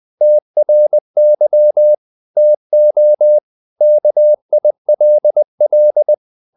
Morse code, heard as a keyed tone, about 20 wpm, 600 Hz. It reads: TRY TO KILL
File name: the old man and the sea part 2 20wpm